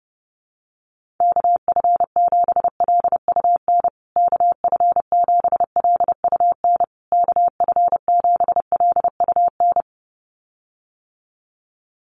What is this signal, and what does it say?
Morse code, heard as a keyed tone, about 30 wpm, 700 Hz. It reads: KF7LUN KF7LUN KF7LUN
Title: USA Call Signs Repeated 3x 30wpm